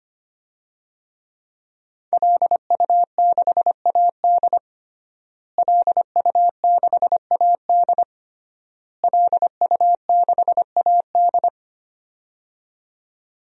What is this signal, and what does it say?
Morse code, heard as a keyed tone, about 25 wpm, 700 Hz. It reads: LU6AD LU6AD LU6AD
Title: DXCC-100 Call Signs - 5 Characters - Repeated 3x - 25wpm